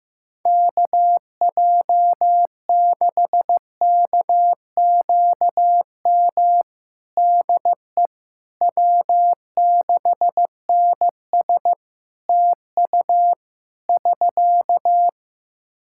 Morse code, heard as a keyed tone, about 15 wpm, 700 Hz. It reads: KJ6KQM DE W6NS TU <SK>
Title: Practice QSO - Exchange 4 - Ending - 15wpm